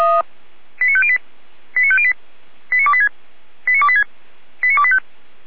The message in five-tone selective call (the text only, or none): 89508 89508 89286 89286 89285